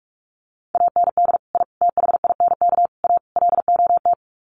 Morse code, heard as a keyed tone, about 44 wpm, 700 Hz. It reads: AND I THINK A LOT
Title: Sets of 5 Words Encore2 45wpm